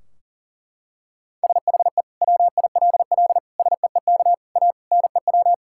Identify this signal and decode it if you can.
Morse code, heard as a keyed tone, about 40 wpm, 700 Hz. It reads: SHE WILL SEEK A NEW
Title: Sentences from Top 1600 Words 40wpm